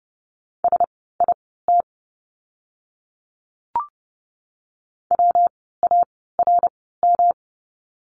Morse code, heard as a keyed tone, about 30 wpm, spaced as for 17 wpm, 700 Hz. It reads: SIT WARM
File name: course / 30:10wpm / 115 Lesson 19f - Letter w - Words 30wpm